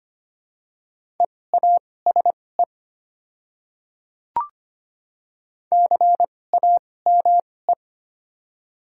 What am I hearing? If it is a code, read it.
Morse code, heard as a keyed tone, about 25 wpm, spaced as for 18 wpm, 700 Hz. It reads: EASE CAME
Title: 2x Character Spacing - Top 300 Words - 25wpm